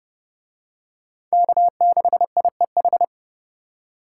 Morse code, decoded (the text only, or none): K6IEH